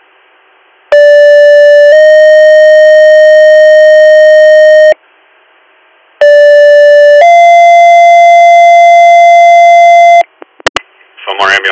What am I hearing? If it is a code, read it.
Two-tone page, heard as a one-second tone, then a three-second tone, about 580 then 620 Hz, about 580 then 690 Hz, with speech. It is heard from 0.9 to 10.2 s.